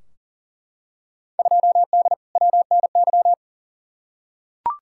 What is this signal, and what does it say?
Morse code, heard as a keyed tone, about 38 wpm, 700 Hz. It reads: 2D WNY
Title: Field Day Exchanges - 40wpm